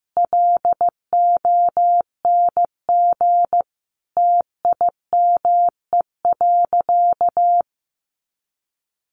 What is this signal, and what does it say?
Morse code, heard as a keyed tone, about 15 wpm, 700 Hz. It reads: LONG TIME.
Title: Sentences from Top 100 Words 15wpm